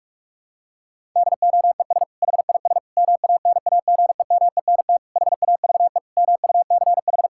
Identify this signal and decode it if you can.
Morse code, heard as a keyed tone, about 46 wpm, 700 Hz. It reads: DOES HIS MANAGEMENT HAVE MUCH
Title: Sentences from Top 1000 Words - Review Part 6 - 45wpm